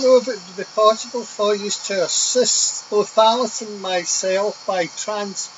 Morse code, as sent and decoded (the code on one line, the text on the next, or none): none